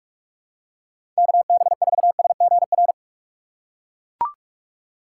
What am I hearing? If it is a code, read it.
Morse code, heard as a keyed tone, about 42 wpm, 700 Hz. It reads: KB4SGR
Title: USA Call Signs - 6 Characters - 4x Character Spacing - 45wpm